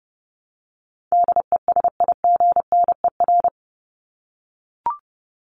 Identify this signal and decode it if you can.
Morse code, heard as a keyed tone, about 30 wpm, 700 Hz. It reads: DESIGNER
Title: Top 1501-1750 Words 30wpm